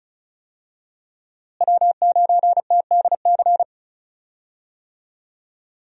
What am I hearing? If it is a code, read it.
Morse code, heard as a keyed tone, about 36 wpm, 700 Hz. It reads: W9TDC
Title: USA Call Signs - 5 Characters - Repeated 2x - 35wpm